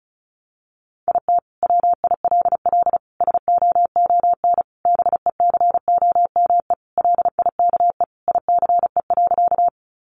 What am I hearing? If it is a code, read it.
Morse code, heard as a keyed tone, about 34 wpm, 700 Hz. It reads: IT WILL SOON BECOME LIKE ICE.
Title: Sentences from Top 1000 Words 35wpm